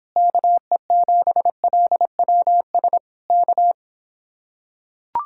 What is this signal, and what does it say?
Morse code, heard as a keyed tone, about 26 wpm, 700 Hz. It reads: KE7LWS K